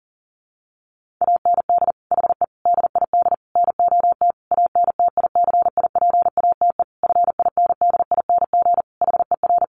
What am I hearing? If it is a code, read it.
Morse code, heard as a keyed tone, about 40 wpm, 700 Hz. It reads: AND HE DID NOT ANTICIPATE FINDING HER
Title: Sentences from Top 1500 Words 40wpm